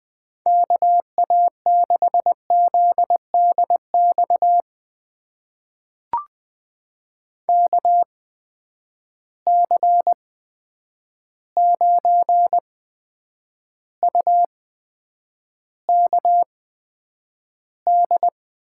Morse code, heard as a keyed tone, about 20 wpm, spaced as for 9 wpm, 700 Hz. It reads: KA6ZDX KC9UKD
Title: USA Call Signs - 6 Characters - 8x Character Spacing - 20wpm